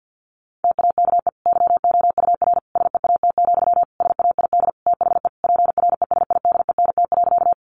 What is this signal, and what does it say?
Morse code, heard as a keyed tone, about 50 wpm, 700 Hz. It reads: TAKE YOUR SEAT, SAID THE PRESIDENT.
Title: Sentences from Top 1500 Words 50wpm